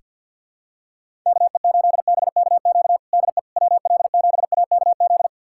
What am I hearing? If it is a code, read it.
Morse code, heard as a keyed tone, about 50 wpm, 690 Hz. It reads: KE8BKQ DE WL7AKZ
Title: USA Callsign DE Callsign - 6 Characters - 50wpm